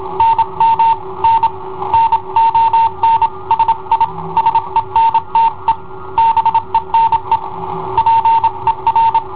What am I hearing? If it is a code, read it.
Morse code, heard as a keyed tone, about 27 wpm, 910 Hz. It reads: NM N NON SI SENTE BENE PER
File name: i7ohp / cassetta i7ohp B